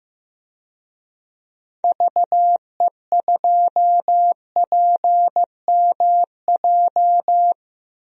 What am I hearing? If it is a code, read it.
Morse code, heard as a keyed tone, about 15 wpm, 700 Hz. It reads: VE2PMJ
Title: DXCC-100 Call Signs - 6 Characters - 15wpm